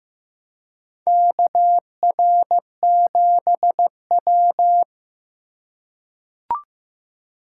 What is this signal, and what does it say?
Morse code, heard as a keyed tone, about 15 wpm, 700 Hz. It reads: KR7W
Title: K1USN SST QSO Practice - 15wpm